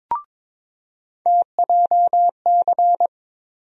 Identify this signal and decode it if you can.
Morse code, heard as a keyed tone, about 21 wpm, 700 Hz. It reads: TJC
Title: USA Call Sign Postfixes - 3 Letters 22wpm